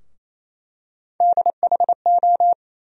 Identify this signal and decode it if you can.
Morse code, heard as a keyed tone, about 28 wpm, 700 Hz. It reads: DHO
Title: USA Call Sign Postfixes - 3 Letters 28wpm